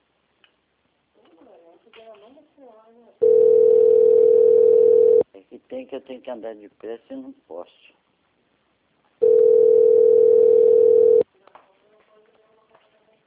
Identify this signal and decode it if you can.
Ringback tone (North America), 440 + 480 Hz, with speech.
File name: out- - - - - .881